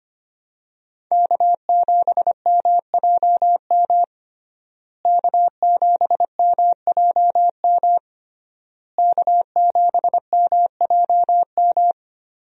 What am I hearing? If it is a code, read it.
Morse code, heard as a keyed tone, about 25 wpm, 700 Hz. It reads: K7MJM K7MJM K7MJM
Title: USA Call Signs - 5 Characters - Repeated 3x - 25wpm